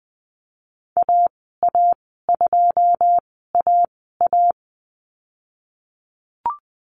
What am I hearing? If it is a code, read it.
Morse code, heard as a keyed tone, about 20 wpm, spaced as for 15 wpm, 700 Hz. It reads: AA2AA
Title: 082 Lesson 14g - Number 2 - Call Signs 20wpm